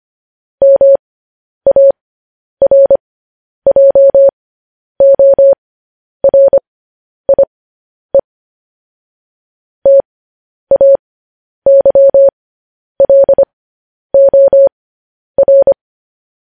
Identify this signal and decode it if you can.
Morse code, heard as a keyed tone, about 25 wpm, spaced as for 10 wpm, 550 Hz. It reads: MARJORIE TAYLOR